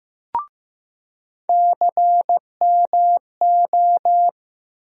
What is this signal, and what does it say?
Morse code, heard as a keyed tone, about 15 wpm, 700 Hz. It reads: CMO
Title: USA Call Sign Postfixes - 3 Letters 15wpm